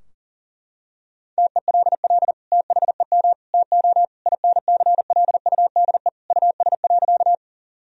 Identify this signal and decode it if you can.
Morse code, heard as a keyed tone, about 40 wpm, 700 Hz. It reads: TELL THEM TO INCLUDE US.
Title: Sentences from Top 1000 Words - Review Part 3 - 40wpm